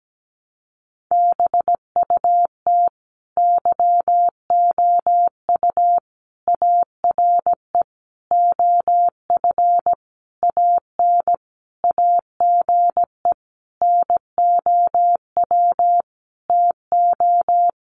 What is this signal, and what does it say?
Morse code, heard as a keyed tone, about 17 wpm, 700 Hz. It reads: BUT YOU ARE OF AN AGE NOW TO
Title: Sentences from Top 1300 Words 17wpm